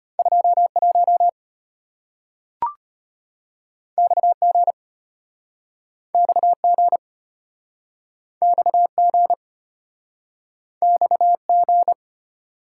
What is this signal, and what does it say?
Morse code, heard as a keyed tone, about 34 wpm, 700 Hz. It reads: 21 XG XG XG XG